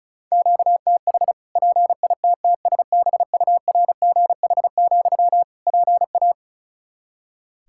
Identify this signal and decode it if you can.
Morse code, heard as a keyed tone, about 34 wpm, 700 Hz. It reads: QTH PITTSBURGH, PA
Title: Practice QSO - Exchange 1 - Beginning - 35wpm